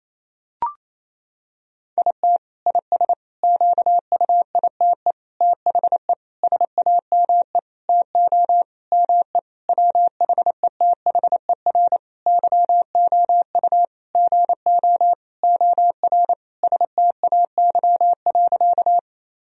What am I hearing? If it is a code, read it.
Morse code, heard as a keyed tone, about 28 wpm, 700 Hz. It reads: IT IS QUITE THE SAME TO ME WHETHER YOU GO OR STAY.
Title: Sentences from Top 1000 Words - Review Part 3 - 28wpm